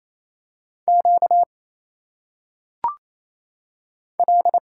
Morse code, heard as a keyed tone, about 27 wpm, 700 Hz. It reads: Q L
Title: Single Letters - Phonetic Pronunciation 28wpm